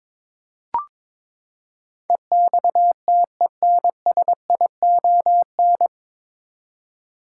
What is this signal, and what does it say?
Morse code, heard as a keyed tone, about 22 wpm, 700 Hz. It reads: EXTENSION
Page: Morse Code Ninja practice 22wpm